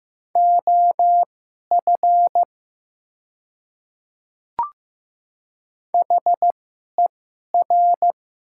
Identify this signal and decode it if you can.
Morse code, heard as a keyed tone, about 15 wpm, spaced as for 11 wpm, 700 Hz. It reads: OF HER
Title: 2x Character Spacing - Top 300 Words - 15wpm